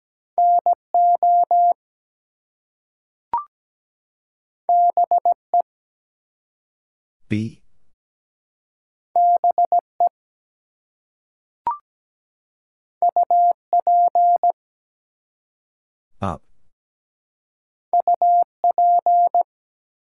Morse code, heard as a keyed tone, about 17 wpm, 700 Hz. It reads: NO BE BE UP UP
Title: Two Letter Words - 17wpm